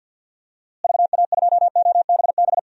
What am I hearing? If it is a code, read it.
Morse code, heard as a keyed tone, about 47 wpm, 700 Hz. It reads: VA2OBB
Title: DXCC-100 Call Signs - 6 Characters - Repeated 3x - 50wpm